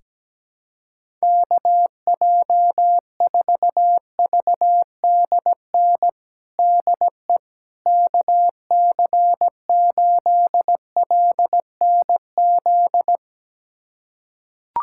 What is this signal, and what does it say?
Morse code, heard as a keyed tone, about 17 wpm, 700 Hz. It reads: KJ4VDN DE KC8LNZ
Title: USA Callsign DE Callsign - 6 Characters - 17wpm